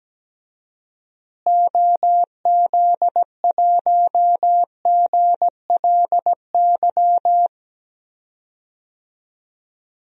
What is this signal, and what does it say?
Morse code, heard as a keyed tone, about 17 wpm, 700 Hz. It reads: OZ1GLY
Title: DXCC-100 Call Signs - 6 Characters - Repeated 2x - 17wpm